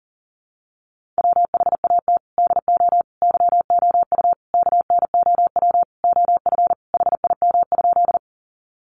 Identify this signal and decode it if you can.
Morse code, heard as a keyed tone, about 40 wpm, 700 Hz. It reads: WHAT DO YOU KNOW OF HIM?
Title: Sentences from Top 100 Words 40wpm